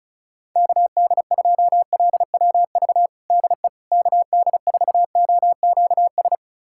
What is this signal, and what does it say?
Morse code, heard as a keyed tone, about 36 wpm, 700 Hz. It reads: KD2LWV DE KD4OQS
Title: USA Callsign DE Callsign - 6 Characters - 35wpm